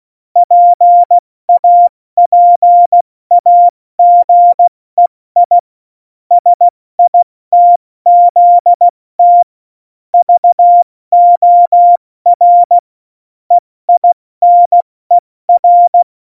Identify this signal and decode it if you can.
Morse code, heard as a keyed tone, about 16 wpm, 700 Hz. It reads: PAPAGEI SITZT VOR EINER
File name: Morsekurs / Witze / Witz 23. Oktober 2018, 70 F80